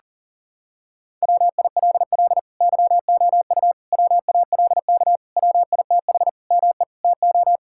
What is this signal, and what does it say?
Morse code, heard as a keyed tone, about 38 wpm, 700 Hz. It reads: WILL YOU WALK WITH ME TO
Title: Sentences from Top 1000 Words - Review Part 6 - 40wpm